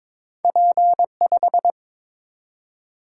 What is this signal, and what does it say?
Morse code, heard as a keyed tone, about 22 wpm, 700 Hz. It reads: P5
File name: practice / 22wpm / Two Letter-Number Combinations - 22wpm